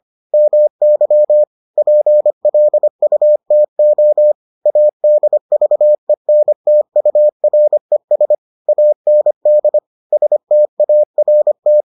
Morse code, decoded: MY PLUTO ADVENTURES AND START

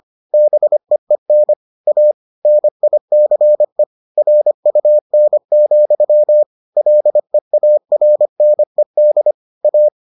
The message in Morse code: BEEN A NICE RUN, LEARNED A

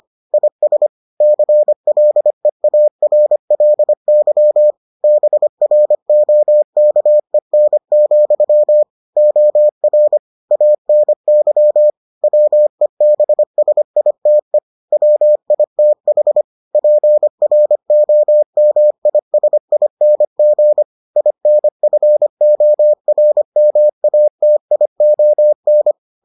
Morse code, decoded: IS CLEARLY BROKEN, OR ANY WEBSITE WITH PROMISING INFORMATION

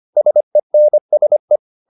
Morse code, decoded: SENSE

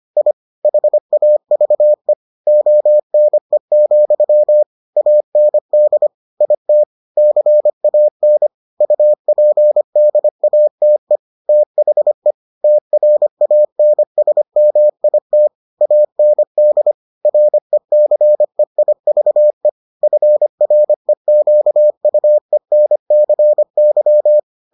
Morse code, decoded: I HAVE ONE, AND IT CAN UPDATE THE TRANSMIT AND RECEIVE FREQUENCY

